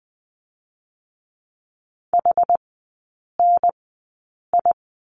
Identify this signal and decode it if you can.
Morse code, heard as a keyed tone, about 20 wpm, spaced as for 8 wpm, 700 Hz. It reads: HNI
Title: 081 Lesson 14e - Number 2 - Three Characters 20wpm